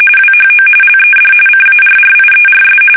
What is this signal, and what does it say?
RTTY, about 45.45 baud. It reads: WHO LED CARRIERS
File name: fsk850 / battleshipnj